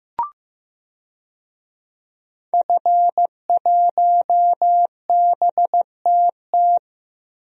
Morse code, heard as a keyed tone, about 16 wpm, 700 Hz. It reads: F1BTT
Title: DXCC-100 Call Signs - 5 Characters - Repeated 3x - 15wpm